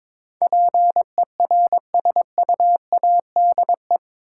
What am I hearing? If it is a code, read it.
Morse code, heard as a keyed tone, about 22 wpm, 700 Hz. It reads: PERSUADE
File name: practice / 22wpm / Eight Letter Words - 22wpm